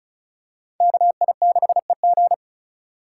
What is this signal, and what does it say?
Morse code, heard as a keyed tone, about 36 wpm, 700 Hz. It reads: KI6EG
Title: USA Call Signs - 5 Characters - 3x Character Spacing - 35wpm